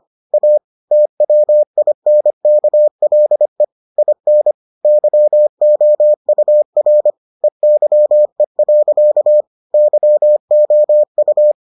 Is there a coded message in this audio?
Yes: Morse code from 0.3 to 11.5 s, heard as a keyed tone, about 25 wpm, 600 Hz. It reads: A TWINKLE IN YOUR EYE. YOU